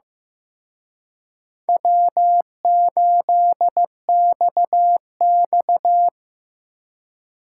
Morse code, heard as a keyed tone, about 16 wpm, 700 Hz. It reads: W8XX
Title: Speed Racing - USA Call Signs 15wpm